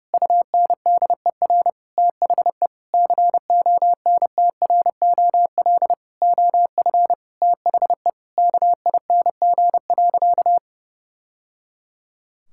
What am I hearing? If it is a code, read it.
Morse code, heard as a keyed tone, about 32 wpm, 700 Hz. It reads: UNDER THE CONTROL OF THE KING.
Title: Sentences from Top 1000 Words - Review Part 5 - 30wpm